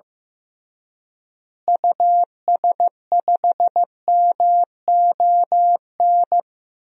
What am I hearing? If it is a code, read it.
Morse code, heard as a keyed tone, about 14 wpm, 700 Hz. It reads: US5MON
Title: DXCC-100 Call Signs - 6 Characters - Repeated 3x - 15wpm